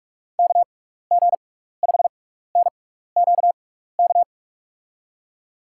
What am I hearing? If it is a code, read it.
Morse code, heard as a keyed tone, about 44 wpm, spaced as for 15 wpm, 700 Hz. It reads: KG5NQK
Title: USA Call Signs - 6x Character Spacing - 45wpm